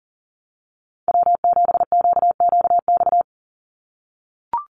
Morse code, heard as a keyed tone, about 40 wpm, 700 Hz. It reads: W7QQX
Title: USA Call Signs 40wpm